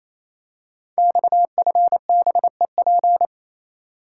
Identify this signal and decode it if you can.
Morse code, heard as a keyed tone, about 27 wpm, 700 Hz. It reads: XFBEP
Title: Code Groups - 5 Letter:Numbers - 28wpm